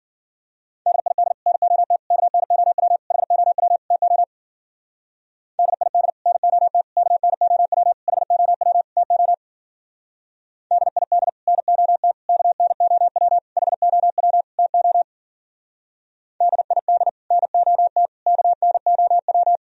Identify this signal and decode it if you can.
Morse code, heard as a keyed tone, about 51 wpm, 700 Hz. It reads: DID NOT KNOW HOW TO DID NOT KNOW HOW TO DID NOT KNOW HOW TO DID NOT KNOW